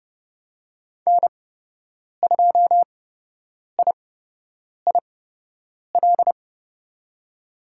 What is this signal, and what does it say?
Morse code, heard as a keyed tone, about 30 wpm, spaced as for 8 wpm, 700 Hz. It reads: N2IIL
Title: USA Call Signs - 5 Characters - 8x Character Spacing - 30wpm